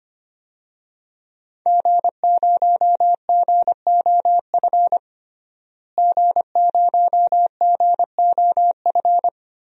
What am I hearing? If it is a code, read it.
Morse code, heard as a keyed tone, about 25 wpm, 700 Hz. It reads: G0GOF G0GOF
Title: DXCC-100 Call Signs - 5 Characters - Repeated 2x - 25wpm